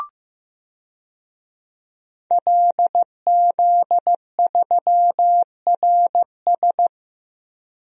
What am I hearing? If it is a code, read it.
Morse code, heard as a keyed tone, about 15 wpm, 700 Hz. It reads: LZ3RS
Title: DXCC-100 Call Signs - 5 Characters - 15wpm